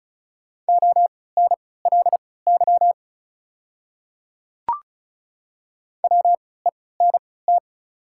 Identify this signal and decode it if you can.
Morse code, heard as a keyed tone, about 36 wpm, spaced as for 20 wpm, 690 Hz. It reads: ONLY WENT
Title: 3x Character Spacing - Top 300 Words - 35wpm